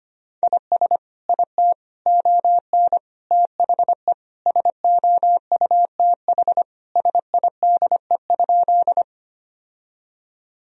Morse code, heard as a keyed tone, about 25 wpm, 700 Hz. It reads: IS IT ON THE SOUTH SIDE?
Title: Sentences from Top 1000 Words - Review Part 2 - 25wpm